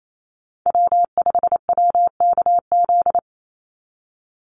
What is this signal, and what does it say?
Morse code, heard as a keyed tone, about 28 wpm, 700 Hz. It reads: W5WKZ
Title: USA Call Signs 28wpm